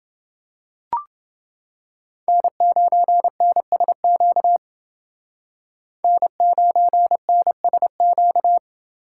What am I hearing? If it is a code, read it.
Morse code, heard as a keyed tone, about 29 wpm, 700 Hz. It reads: N9NSQ N9NSQ